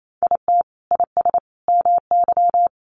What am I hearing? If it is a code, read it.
Morse code, heard as a keyed tone, about 28 wpm, 700 Hz. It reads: IT IS MY
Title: Sentences from Top 1000 Words 28wpm